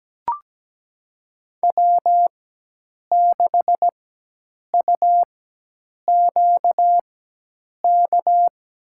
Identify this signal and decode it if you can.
Morse code, heard as a keyed tone, about 17 wpm, spaced as for 8 wpm, 700 Hz. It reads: W6UQK
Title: USA Call Signs - 5 Characters - 4x Character Spacing - 17wpm